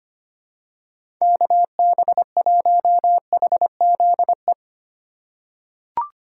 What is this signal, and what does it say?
Morse code, heard as a keyed tone, about 25 wpm, 700 Hz. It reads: KB1HZE
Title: USA Call Signs - 6 Characters - 4x Character Spacing - 25wpm